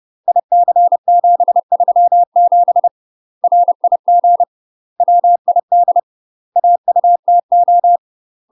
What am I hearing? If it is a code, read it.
Morse code, heard as a keyed tone, about 30 wpm, 700 Hz. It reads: IC737 RIG WID AUTO
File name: UK MASTER QSOs TRAINING / 30WPM / UK MASTER QSOs 0007